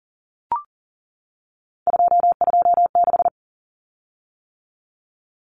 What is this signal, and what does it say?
Morse code, heard as a keyed tone, about 40 wpm, 700 Hz. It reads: 226
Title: Common Cut Numbers - 3-Digit 40wpm